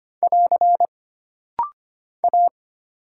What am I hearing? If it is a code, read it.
Morse code, heard as a keyed tone, about 25 wpm, 700 Hz. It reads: <AR> A
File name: Single Character (Letter:Number:Punctuation:Prosign) - Rapid-Fire - 25wpm